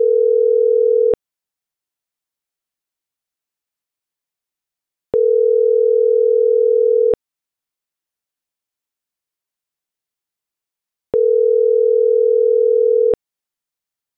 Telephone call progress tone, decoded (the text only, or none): ringback tone